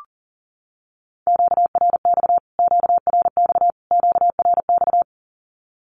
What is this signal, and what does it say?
Morse code, heard as a keyed tone, about 40 wpm, 700 Hz. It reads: QRX QRX QRX
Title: Q-Codes Repeated 3x 40wpm